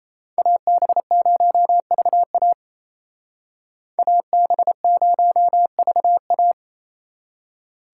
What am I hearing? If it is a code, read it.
Morse code, heard as a keyed tone, about 36 wpm, 700 Hz. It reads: AB0VA AB0VA